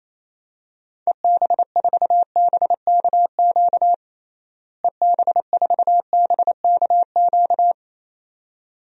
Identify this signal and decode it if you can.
Morse code, heard as a keyed tone, about 28 wpm, 700 Hz. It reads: EB4BKQ EB4BKQ
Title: DXCC-100 Call Signs - 6 Characters - Repeated 2x - 28wpm